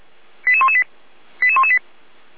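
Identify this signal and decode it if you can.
Five-tone selective call (ZVEI), heard as quick runs of five tones, tones of about 70 ms. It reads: 88118 88118
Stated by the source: ZVEI five-tone call received by SDR.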